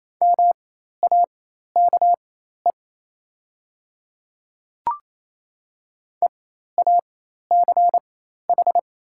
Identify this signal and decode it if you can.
Morse code, heard as a keyed tone, about 28 wpm, spaced as for 13 wpm, 700 Hz. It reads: MAKE EACH
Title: 4x Character Spacing - Top 100 Words - 28wpm